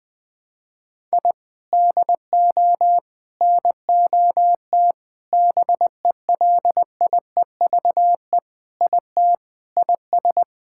Morse code, heard as a keyed tone, about 20 wpm, 700 Hz. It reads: I DO NOT BELIEVE IT IS
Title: Sentences with 5-Word Phrases - Spoken Beginning - 20wpm